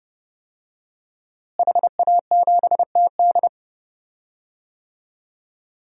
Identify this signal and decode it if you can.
Morse code, heard as a keyed tone, about 31 wpm, 700 Hz. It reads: HA7TD
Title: DXCC-100 Call Signs - 5 Characters - Repeated 3x - 30wpm